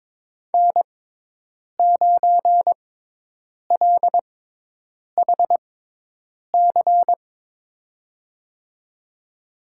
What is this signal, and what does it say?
Morse code, heard as a keyed tone, about 22 wpm, spaced as for 8 wpm, 700 Hz. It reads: N9LHC
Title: USA Call Signs - 5 Characters - 6x Character Spacing - 22wpm